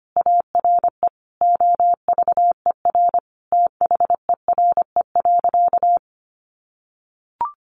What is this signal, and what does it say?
Morse code, heard as a keyed tone, about 25 wpm, 700 Hz. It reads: ARE OVER THERE.
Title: Sentences from Top 600 Words 25wpm